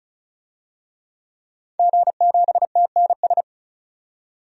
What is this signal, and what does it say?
Morse code, heard as a keyed tone, about 34 wpm, 700 Hz. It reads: G7TNS
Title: DXCC-100 Call Signs - 5 Characters - 35wpm